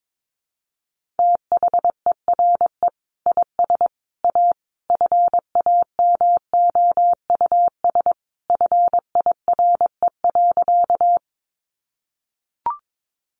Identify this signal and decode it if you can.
Morse code, heard as a keyed tone, about 22 wpm, 700 Hz. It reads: THERE IS A FAMOUS FIRE.
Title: Sentences from Top 1500 Words 22wpm